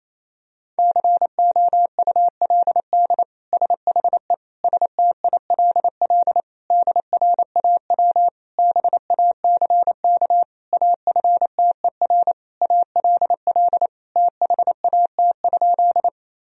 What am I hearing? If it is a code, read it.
Morse code, heard as a keyed tone, about 28 wpm, 700 Hz. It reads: COULD SHE STILL DRAW BACK AFTER ALL THAT?
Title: Sentences from Top 1000 Words - Review Part 1 - 28wpm